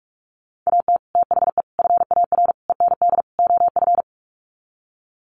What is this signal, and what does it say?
Morse code, heard as a keyed tone, about 44 wpm, 700 Hz. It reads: AT THE FAR END OF
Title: Sets of 5 Words 45wpm